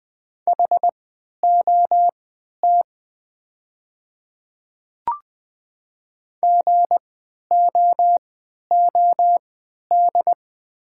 Morse code, heard as a keyed tone, about 20 wpm, spaced as for 11 wpm, 700 Hz. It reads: HOT GOOD
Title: 3x Character Spacing - Top 500 Words - 20wpm